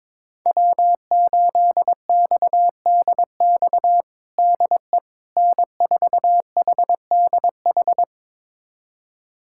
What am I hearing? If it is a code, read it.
Morse code, heard as a keyed tone, about 22 wpm, 700 Hz. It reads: W8XDX DE N4HDH
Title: USA Callsign DE Callsign - 5 Characters - 22wpm